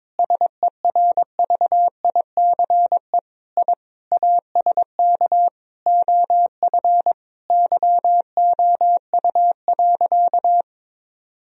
Morse code, heard as a keyed tone, about 22 wpm, 700 Hz. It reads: SERVICE I ASK OF YOU.
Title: Sentences from Top 1000 Words - Review Part 4 - 22wpm